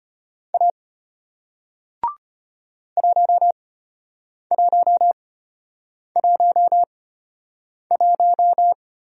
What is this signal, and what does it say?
Morse code, heard as a keyed tone, about 33 wpm, spaced as for 8 wpm, 700 Hz. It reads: A 1111